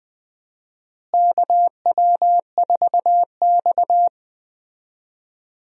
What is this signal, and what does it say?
Morse code, heard as a keyed tone, about 20 wpm, 700 Hz. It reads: KW4X
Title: USA Call Signs - 4 Characters - Repeated 2x - 20wpm